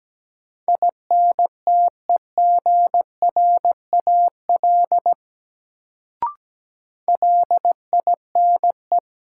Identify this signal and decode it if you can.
Morse code, heard as a keyed tone, about 17 wpm, 700 Hz. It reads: INTEGRAL LINE
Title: Sets of 5 Words - New Word Drill - 17wpm